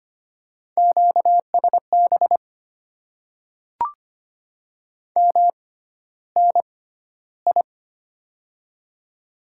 Morse code, heard as a keyed tone, about 25 wpm, spaced as for 13 wpm, 700 Hz. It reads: QSB MNI
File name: Top 90 QSO Elements - 6x Character Spacing - 25wpm